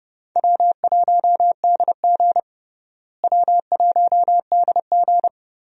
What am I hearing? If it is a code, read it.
Morse code, heard as a keyed tone, about 30 wpm, 700 Hz. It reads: W1DG W1DG
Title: USA Call Signs - 4 Characters - Repeated 3x - 30wpm